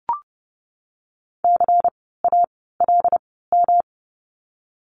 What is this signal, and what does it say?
Morse code, heard as a keyed tone, about 30 wpm, spaced as for 17 wpm, 700 Hz. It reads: CALM
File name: course / 30:10wpm / 108 Lesson 18f - Letter m - Words 30wpm